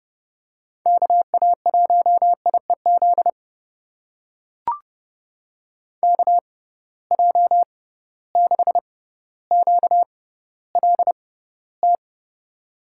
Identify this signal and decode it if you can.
Morse code, heard as a keyed tone, about 30 wpm, spaced as for 15 wpm, 700 Hz. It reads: KA1IEZ KJ6QLT
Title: USA Call Signs - 6 Characters - 6x Character Spacing - 30wpm